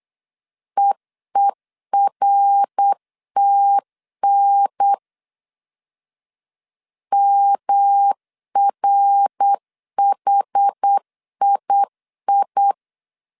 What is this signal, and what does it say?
Morse code, heard as a keyed tone, about 8 wpm, 800 Hz. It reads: EERTN MRHII